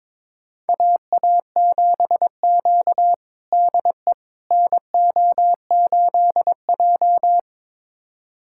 Morse code, heard as a keyed tone, about 22 wpm, 700 Hz. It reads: AA7Q DE NO8J
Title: USA Callsign DE Callsign - 4 Characters - 22wpm